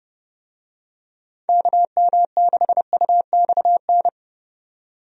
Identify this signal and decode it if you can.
Morse code, heard as a keyed tone, about 32 wpm, 700 Hz. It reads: KM6UXN
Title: USA Call Signs - 6 Characters - 30wpm